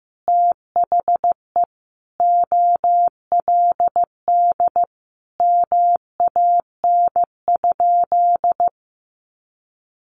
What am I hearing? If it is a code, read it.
Morse code, heard as a keyed tone, about 15 wpm, 700 Hz. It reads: THE OLD MAN?
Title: Sentences from Top 1100 Words 15wpm